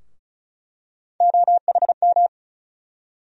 Morse code, heard as a keyed tone, about 36 wpm, 700 Hz. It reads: OHM
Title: USA Call Sign Postfixes - 3 Letters 35wpm